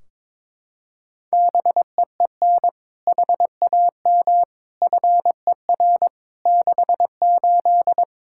Morse code, heard as a keyed tone, about 22 wpm, 700 Hz. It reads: BEEN HAM FER 68
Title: K1USN SST QSO Practice 22wpm